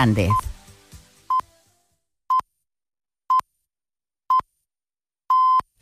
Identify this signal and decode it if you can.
Radio time signal (time pips), with speech. It is heard from 0.3 to 5.6 s.